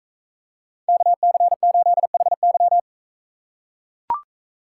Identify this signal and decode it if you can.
Morse code, heard as a keyed tone, about 44 wpm, 700 Hz. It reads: KC8HY